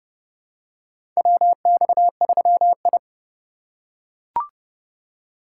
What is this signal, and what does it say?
Morse code, heard as a keyed tone, about 30 wpm, 700 Hz. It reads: WX3I
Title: USA Call Signs - 4 Characters - 4x Character Spacing - 30wpm